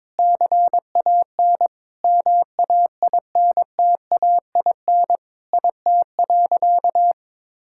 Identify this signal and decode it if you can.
Morse code, heard as a keyed tone, about 22 wpm, 700 Hz. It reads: CAN MAINTAIN IT.
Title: Sentences from Top 1000 Words - Review Part 5 - 22wpm